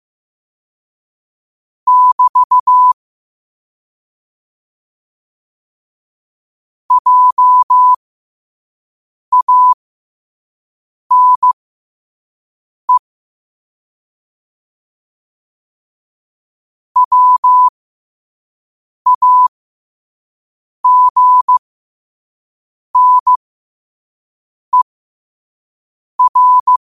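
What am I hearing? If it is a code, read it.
Morse code, heard as a keyed tone, about 15 wpm, spaced as for 5 wpm, 1000 Hz. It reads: <BT> JANE WAGNER
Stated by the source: Morse sent at 5 wpm.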